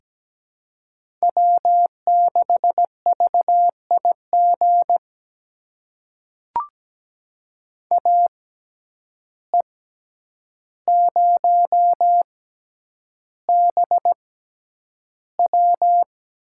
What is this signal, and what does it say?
Morse code, heard as a keyed tone, about 17 wpm, spaced as for 9 wpm, 700 Hz. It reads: W6VIG AE0BW
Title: USA Call Signs - 5 Characters - 6x Character Spacing - 17wpm